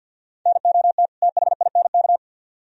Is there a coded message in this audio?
Yes: Morse code from 0.5 to 2.2 s, heard as a keyed tone, about 64 wpm, 700 Hz. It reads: NOT THINK